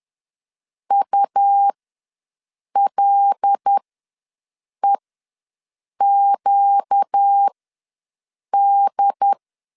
Morse code, heard as a keyed tone, about 10 wpm, spaced as for 6 wpm, 800 Hz. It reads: ULEQD
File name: LEZIONI / 5 LBVC / 5 L ws 10wpm 3636